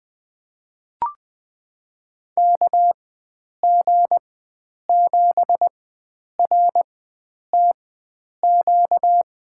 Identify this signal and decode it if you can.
Morse code, heard as a keyed tone, about 20 wpm, spaced as for 9 wpm, 700 Hz. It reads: KG7RTQ
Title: USA Call Signs - 6 Characters - 4x Character Spacing - 20wpm